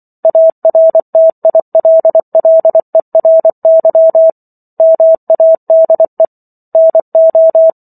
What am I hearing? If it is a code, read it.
Morse code, heard as a keyed tone, about 24 wpm, 650 Hz. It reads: ARTILLERY MADE NO